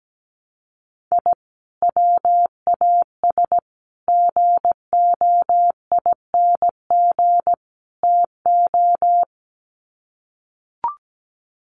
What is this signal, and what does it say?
Morse code, heard as a keyed tone, about 17 wpm, 700 Hz. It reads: I WAS GOING TO
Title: Sets of 4 Words 17wpm